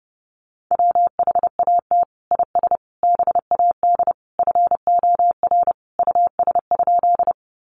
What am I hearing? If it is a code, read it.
Morse code, heard as a keyed tone, about 30 wpm, 700 Hz. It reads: WHAT IS BAD FOR US?
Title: Sentences from Top 700 Words 30wpm